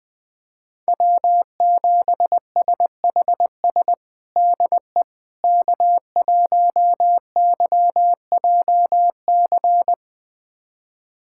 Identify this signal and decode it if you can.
Morse code, heard as a keyed tone, about 20 wpm, 700 Hz. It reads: W7SHS DE K1YJC
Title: USA Callsign DE Callsign - 5 Characters - 20wpm